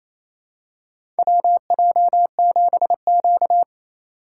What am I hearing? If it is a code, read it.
Morse code, heard as a keyed tone, about 28 wpm, 690 Hz. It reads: WJ7Q
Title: USA Call Signs - 2x Character Spacing - 28wpm